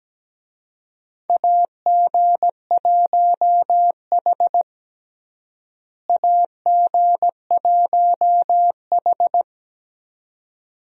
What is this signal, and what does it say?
Morse code, heard as a keyed tone, about 17 wpm, 700 Hz. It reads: AG1H AG1H
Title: USA Call Signs - 4 Characters - Repeated 2x - 17wpm